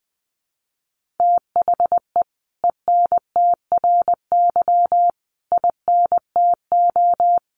Morse code, heard as a keyed tone, about 20 wpm, 700 Hz. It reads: THE ENTRY INTO
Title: Sentences from Top 1500 Words 20wpm